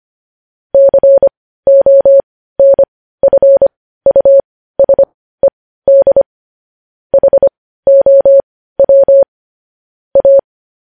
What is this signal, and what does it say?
Morse code, heard as a keyed tone, about 25 wpm, spaced as for 15 wpm, 550 Hz. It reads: CONFUSED HOW A
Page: News Headlines in Morse Code at 15 WPM